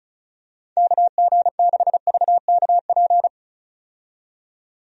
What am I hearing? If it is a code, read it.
Morse code, heard as a keyed tone, about 34 wpm, 700 Hz. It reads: KG6VKP
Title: USA Call Signs - 6 Characters - 35wpm